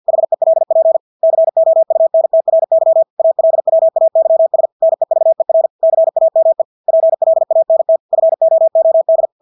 Morse code, heard as a keyed tone, about 50 wpm, 650 Hz. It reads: HELP COUNTRY ALWAYS NEVER CAME PLANT FOOD
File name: Engelske ord 50 wpm